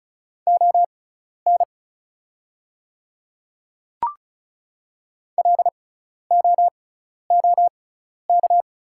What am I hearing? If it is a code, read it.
Morse code, heard as a keyed tone, about 36 wpm, spaced as for 12 wpm, 700 Hz. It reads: ON LOOK
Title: 6x Character Spacing - Top 100 Words - 35wpm